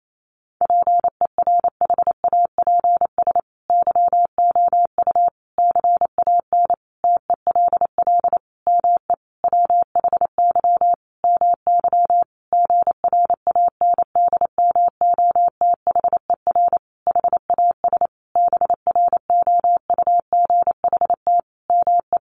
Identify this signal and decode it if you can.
Morse code, heard as a keyed tone, about 28 wpm, 700 Hz. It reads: PERHAPS YOU CAN TELL ME WHY MY GRANDMOTHER HAS BROUGHT ME
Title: Sentences from Top 1200 Words 28wpm